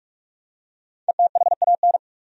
Morse code, heard as a keyed tone, about 44 wpm, 700 Hz. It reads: ETHAN
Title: USA Common Names - 45wpm